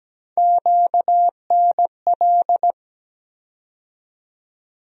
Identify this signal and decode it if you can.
Morse code, heard as a keyed tone, about 17 wpm, 700 Hz. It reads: QNL
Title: USA Call Sign Postfixes - 3 Letters 17wpm